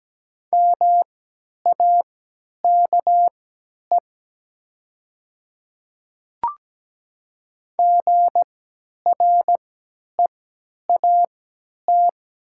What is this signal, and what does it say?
Morse code, heard as a keyed tone, about 17 wpm, spaced as for 10 wpm, 700 Hz. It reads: MAKE GREAT
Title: 3x Character Spacing - Top 300 Words - 17wpm